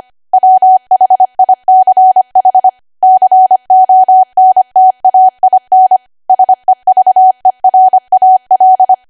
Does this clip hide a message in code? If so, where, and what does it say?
Yes: Morse code from 0.3 to 8.9 s, heard as a keyed tone, about 25 wpm, 750 Hz. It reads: WHICH CONTAIN SEVERAL